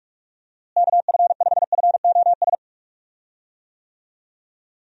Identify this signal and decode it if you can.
Morse code, heard as a keyed tone, about 46 wpm, 700 Hz. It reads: KF5FOS
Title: USA Call Signs - 6 Characters - Repeated 3x - 45wpm